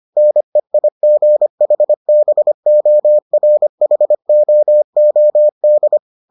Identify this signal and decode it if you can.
Morse code, heard as a keyed tone, about 25 wpm, 600 Hz. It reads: NEIGHBORHOOD